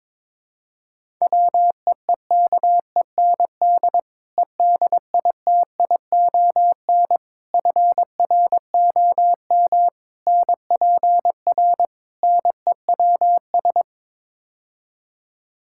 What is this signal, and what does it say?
Morse code, heard as a keyed tone, about 22 wpm, 700 Hz. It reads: WEEKEND EDITION FROM NPR NEWS
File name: practice / 22wpm / Build A Phrase - 5-Word Phrases - 22wpm